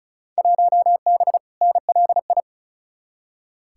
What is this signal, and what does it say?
Morse code, heard as a keyed tone, about 36 wpm, 700 Hz. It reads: 1B NLI
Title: Field Day Exchanges Encore - 35wpm